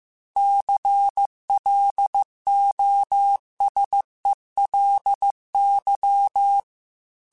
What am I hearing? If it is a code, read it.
Morse code, heard as a keyed tone, about 15 wpm, 800 Hz. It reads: CLOSELY
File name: Technical Subjects / Technical Subjects-mp3s / OR Gate @13